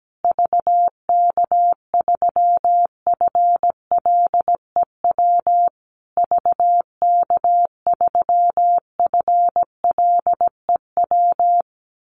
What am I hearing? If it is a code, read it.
Morse code, heard as a keyed tone, about 17 wpm, 700 Hz. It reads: VK3FLEW VK3FLEW
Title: USA Call Signs Repeated 3x 17wpm